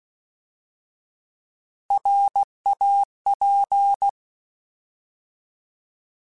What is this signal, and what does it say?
Morse code, heard as a keyed tone, about 16 wpm, 800 Hz. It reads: RAP